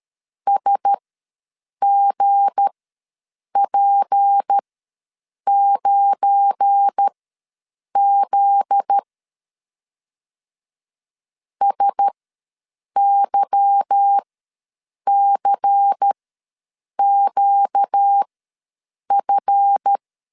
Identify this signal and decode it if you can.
Morse code, heard as a keyed tone, about 13 wpm, spaced as for 7 wpm, 800 Hz. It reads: SGP9Z SYCQF
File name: LEZIONI / 9 28 ws 12wpm 3354